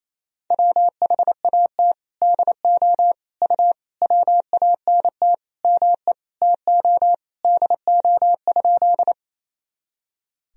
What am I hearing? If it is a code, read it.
Morse code, heard as a keyed tone, about 28 wpm, 700 Hz. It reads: WHAT DO U WANT ME TO DO?
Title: Sentences from Top 200 Words and Abbreviations 28wpm